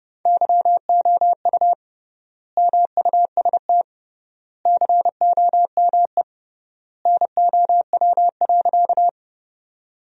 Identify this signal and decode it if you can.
Morse code, heard as a keyed tone, about 30 wpm, 700 Hz. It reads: YOU MUST COME NOW.
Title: 3x Word-Spacing - Sentences from Top 200 Words - 30wpm